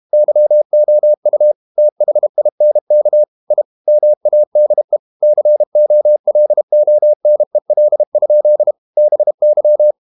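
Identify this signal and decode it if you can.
Morse code, heard as a keyed tone, about 32 wpm, 600 Hz. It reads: YOU THINK I MADE COLONEL? BY
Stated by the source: Morse code practice at 32 wpm from CW med Gnister, reading Ken Follett's novel Never in English.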